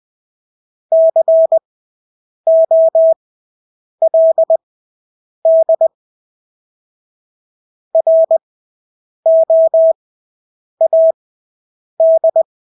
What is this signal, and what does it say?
Morse code, heard as a keyed tone, about 20 wpm, spaced as for 8 wpm, 650 Hz. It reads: COLD ROAD
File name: Engelske ord 20/8 wpm